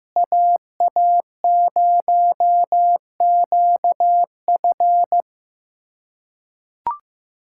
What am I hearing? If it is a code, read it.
Morse code, heard as a keyed tone, about 15 wpm, 700 Hz. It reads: AA0QF
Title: USA Callsign DE Callsign - 5 Characters - 15wpm